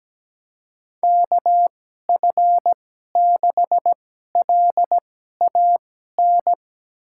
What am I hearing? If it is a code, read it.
Morse code, heard as a keyed tone, about 17 wpm, spaced as for 12 wpm, 700 Hz. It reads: KF6LAN
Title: USA Call Signs - 6 Characters - 2x Character Spacing - 17wpm